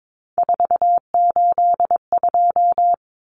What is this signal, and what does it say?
Morse code, heard as a keyed tone, about 22 wpm, 700 Hz. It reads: 482
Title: Common Cut Numbers - 3-Digit 22wpm